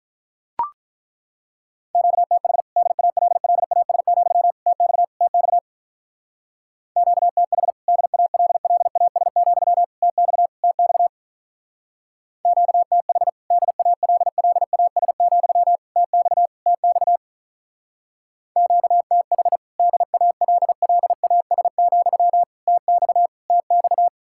Morse code, heard as a keyed tone, about 43 wpm, 700 Hz. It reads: QTH DALLAS, TX TX QTH DALLAS, TX TX QTH DALLAS, TX TX QTH DALLAS, TX TX
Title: Speed-Racing Common QSO Phrases - 35wpm